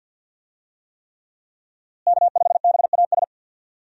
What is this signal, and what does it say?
Morse code, heard as a keyed tone, about 51 wpm, 700 Hz. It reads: K5BAS
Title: USA Call Signs - 5 Characters - Repeated 3x - 50wpm